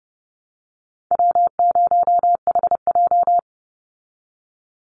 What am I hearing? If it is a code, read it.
Morse code, heard as a keyed tone, about 30 wpm, 700 Hz. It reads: W0HJ
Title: USA Call Signs 30wpm